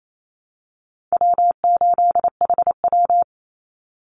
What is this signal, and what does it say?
Morse code, heard as a keyed tone, about 28 wpm, 700 Hz. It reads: W8HW
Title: Call Signs - 28wpm